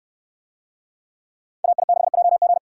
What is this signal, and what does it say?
Morse code, heard as a keyed tone, about 69 wpm, 700 Hz. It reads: AI6JG